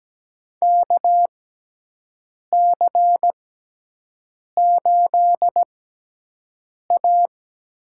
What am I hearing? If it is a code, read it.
Morse code, heard as a keyed tone, about 17 wpm, spaced as for 6 wpm, 700 Hz. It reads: KC8A
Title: USA Call Signs - 6x Character Spacing - 17wpm